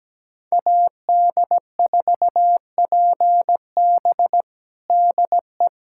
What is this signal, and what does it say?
Morse code, heard as a keyed tone, about 17 wpm, 700 Hz. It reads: AD4PB DE
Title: USA Callsign DE Callsign - 5 Characters - 17wpm